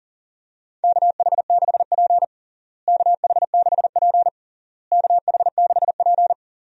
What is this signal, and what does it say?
Morse code, heard as a keyed tone, about 40 wpm, 700 Hz. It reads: KH6P KH6P KH6P
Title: DXCC-100 Call Signs - 4 Characters - Repeated 3x - 40wpm